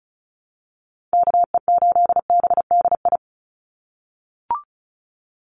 Morse code, heard as a keyed tone, about 34 wpm, 700 Hz. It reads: KE8BDI
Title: USA Call Signs 35wpm